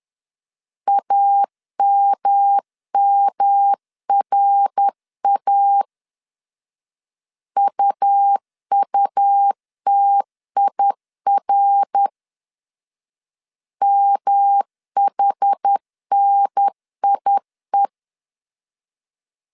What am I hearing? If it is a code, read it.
Morse code, heard as a keyed tone, about 10 wpm, 800 Hz. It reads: AMMRA UUTIR MHNIE
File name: CW / DET / 1 3 1500C S B 10wpm 3335